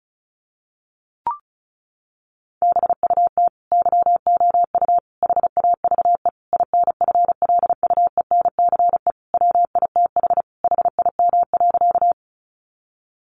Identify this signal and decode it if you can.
Morse code, heard as a keyed tone, about 34 wpm, 700 Hz. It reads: BUT YOU HAVE INFLUENCE WITH HIM.
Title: Sentences from Top 1100 Words 35wpm